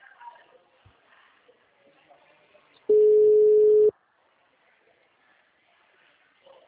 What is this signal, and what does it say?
Ringback tone (Europe, 425 Hz), with speech.
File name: out- -2008- - - .7